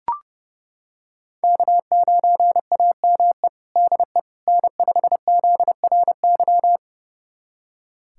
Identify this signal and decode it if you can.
Morse code, heard as a keyed tone, about 32 wpm, 700 Hz. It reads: K9AME DE N5ZRY
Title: USA Callsign DE Callsign - 5 Characters - 30wpm